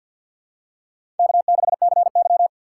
Morse code, heard as a keyed tone, about 53 wpm, 700 Hz. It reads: K6CY
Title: USA Call Signs - 4 Characters - 6x Character Spacing - 50wpm